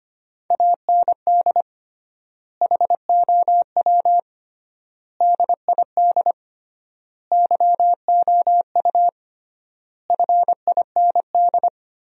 Morse code, heard as a keyed tone, about 25 wpm, 700 Hz. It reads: AND HOW DID YOU FIND
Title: 3x Word-Spacing - Sentences from Top 100 Words - 25wpm